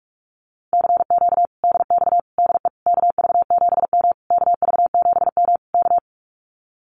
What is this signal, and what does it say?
Morse code, heard as a keyed tone, about 46 wpm, 700 Hz. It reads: CQ DX DE KV7M KV7M K
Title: Common QSO Phrases Encore 45wpm